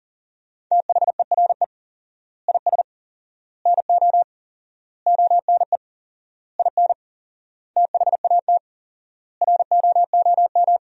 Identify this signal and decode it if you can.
Morse code, heard as a keyed tone, about 40 wpm, 700 Hz. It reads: THERE IS NO ONE IN THAT ROOM
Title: 4x Word-Spacing - Sentences from Top 500 Words - 40wpm